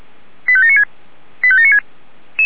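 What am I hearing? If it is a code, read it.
Five-tone selective call (ZVEI), heard as quick runs of five tones, tones of about 80 ms. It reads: 85796 85796
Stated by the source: ZVEI five-tone selective call received by SDR.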